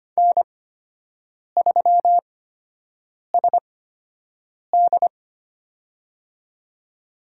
Morse code, heard as a keyed tone, about 25 wpm, spaced as for 7 wpm, 700 Hz. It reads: N3SD
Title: USA Call Signs - 4 Characters - 8x Character Spacing - 25wpm